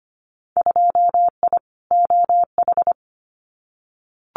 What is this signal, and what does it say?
Morse code, heard as a keyed tone, about 25 wpm, 700 Hz. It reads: 2I OH
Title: Winter Field Day Exchanges 25wpm